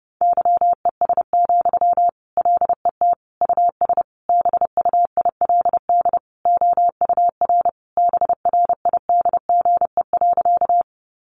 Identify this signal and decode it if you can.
Morse code, heard as a keyed tone, about 30 wpm, 700 Hz. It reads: YES, LET US BUILD OUR BRIDGE.
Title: Sentences from Top 1200 Words 30wpm